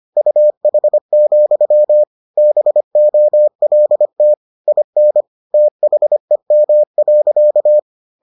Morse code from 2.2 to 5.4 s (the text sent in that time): BOLT IN